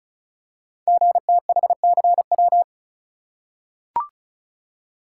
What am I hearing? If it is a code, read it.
Morse code, heard as a keyed tone, about 36 wpm, 700 Hz. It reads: GTHCW
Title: Code Groups - 5 Letters - 35wpm